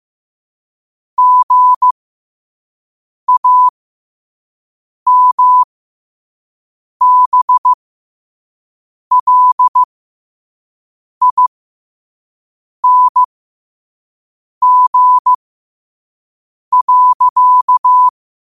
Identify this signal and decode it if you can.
Morse code, heard as a keyed tone, about 15 wpm, spaced as for 5 wpm, 1000 Hz. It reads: GAMBLING.